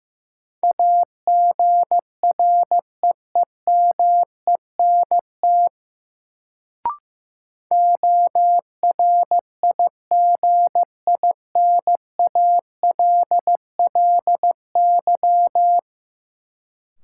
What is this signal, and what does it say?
Morse code, heard as a keyed tone, about 15 wpm, 700 Hz. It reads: AGREEMENT ORIGINALLY
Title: Sentences from Top 1500 Words - New Word Drill Alt - 15wpm